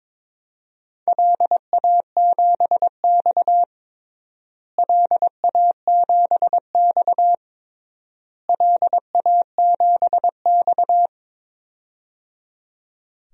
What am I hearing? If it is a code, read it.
Morse code, heard as a keyed tone, about 22 wpm, 700 Hz. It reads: LA7X LA7X LA7X
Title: DXCC-100 Call Signs - 4 Characters - Repeated 3x - 22wpm